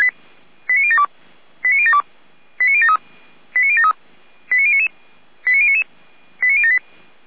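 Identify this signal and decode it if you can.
Five-tone selective call (ZVEI), heard as quick runs of five tones, tones of about 70 ms. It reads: 89072 89072 89073 89073 89099 89099 89078
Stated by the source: Received on SDR, ZVEI five-tone call.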